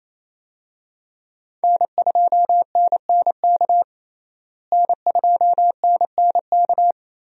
Morse code, heard as a keyed tone, about 28 wpm, 700 Hz. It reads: N2NNK N2NNK
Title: USA Call Signs - 5 Characters - Repeated 2x - 28wpm